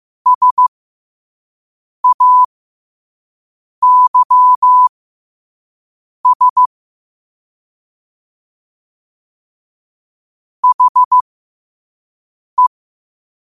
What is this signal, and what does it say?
Morse code, heard as a keyed tone, about 15 wpm, spaced as for 5 wpm, 1000 Hz. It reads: SAYS HE
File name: qotdfiles / 5wpm